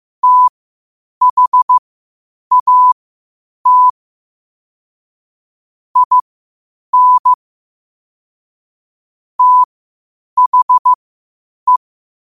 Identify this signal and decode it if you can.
Morse code, heard as a keyed tone, about 15 wpm, spaced as for 8 wpm, 1000 Hz. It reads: THAT IN THE
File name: qotdfiles / 8wpm